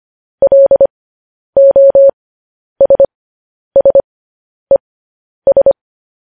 Morse code, heard as a keyed tone, about 25 wpm, spaced as for 10 wpm, 550 Hz. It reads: LOSSES
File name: podcast / News headlines-1x-25x10